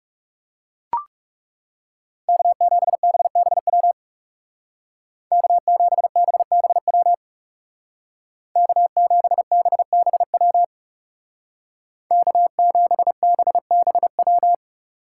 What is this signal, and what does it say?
Morse code, heard as a keyed tone, about 36 wpm, 700 Hz. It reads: K7BBW K7BBW K7BBW K7BBW
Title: Speed Racing - USA 1x3 Call Signs 30wpm